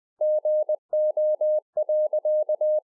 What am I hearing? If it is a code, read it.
Morse code, heard as a keyed tone, about 20 wpm, 610 Hz. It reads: GO.
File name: 2011-04-mp3 / A SMALL TRANCEIVER 3 20WPM